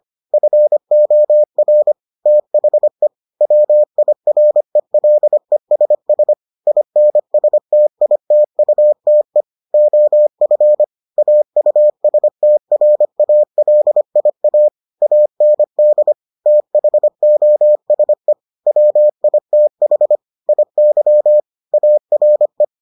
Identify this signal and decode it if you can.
Morse code, heard as a keyed tone, about 25 wpm, 600 Hz. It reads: FOR THE WIRELESS INSTITUTE OF AUSTRALIA AND THOSE WITH IY ARE